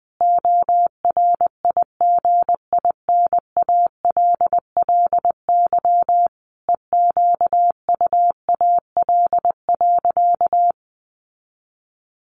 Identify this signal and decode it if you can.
Morse code, heard as a keyed tone, about 20 wpm, 700 Hz. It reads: ORIGINALLY EQUAL.
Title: Sentences from Top 1500 Words 20wpm